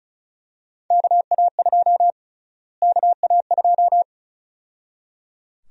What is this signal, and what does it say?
Morse code, heard as a keyed tone, about 36 wpm, 700 Hz. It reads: KA2 KA2
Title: USA Call Sign Prefixes - Repeated 2x - 35wpm